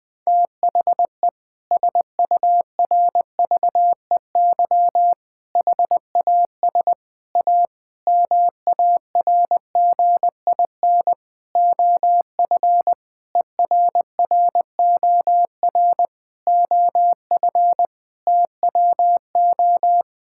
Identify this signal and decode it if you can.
Morse code, heard as a keyed tone, about 21 wpm, 700 Hz. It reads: THE SURVEY HAS A MARGIN OF ERROR OF TWO
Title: Sentences with 6-Word Phrases - 20wpm